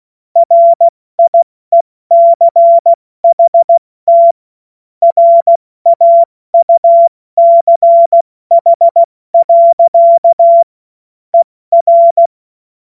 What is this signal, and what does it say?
Morse code, heard as a keyed tone, about 16 wpm, 680 Hz. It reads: RIECHT RAUCH. ER
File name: Morsekurs / Witze / Witz 14. Juni 2022 70 F80